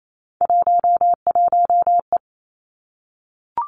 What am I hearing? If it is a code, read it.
Morse code, heard as a keyed tone, about 28 wpm, 690 Hz. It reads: 11E
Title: Three Challenging Letter-Number Combinations 28wpm